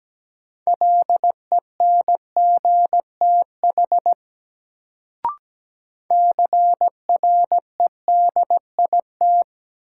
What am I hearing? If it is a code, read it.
Morse code, heard as a keyed tone, about 17 wpm, 700 Hz. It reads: LENGTH CREDIT
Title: Sentences from Top 1000 Words - New Word Drill - 17wpm